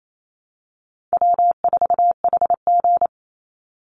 Morse code, heard as a keyed tone, about 28 wpm, 700 Hz. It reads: W4HG
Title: Easy Call Signs Repeated 3x - 28wpm